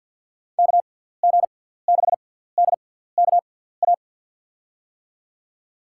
Morse code, heard as a keyed tone, about 47 wpm, spaced as for 17 wpm, 700 Hz. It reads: KG6DKA
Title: USA Call Signs - 6 Characters - 6x Character Spacing - 50wpm